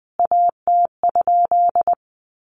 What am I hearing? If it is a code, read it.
Morse code, heard as a keyed tone, about 20 wpm, 700 Hz. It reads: AT?
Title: A Sound Of Thunder 20wpm